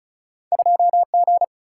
Morse code, heard as a keyed tone, about 36 wpm, 700 Hz. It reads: 2G
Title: Single-Letter Single-Number 35wpm